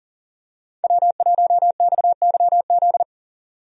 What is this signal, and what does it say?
Morse code, heard as a keyed tone, about 40 wpm, 700 Hz. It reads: W1XYZ
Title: USA Call Signs - 5 Characters - 40wpm